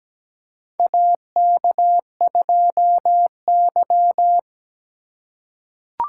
Morse code, heard as a keyed tone, about 17 wpm, 700 Hz. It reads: AK2Y
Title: USA Call Signs - 4 Characters - 2x Character Spacing - 17wpm